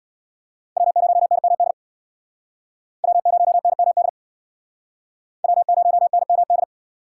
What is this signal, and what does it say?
Morse code, heard as a keyed tone, about 64 wpm, 700 Hz. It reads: W0NMD W0NMD W0NMD